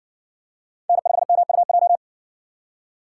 Morse code, heard as a keyed tone, about 60 wpm, 700 Hz. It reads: N5MUJ